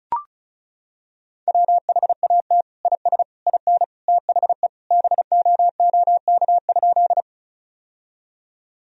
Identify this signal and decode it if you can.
Morse code, heard as a keyed tone, about 36 wpm, 700 Hz. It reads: WHAT IS IN THE BOOK?
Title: Sentences from Top 1000 Words - Review Part 1 - 35wpm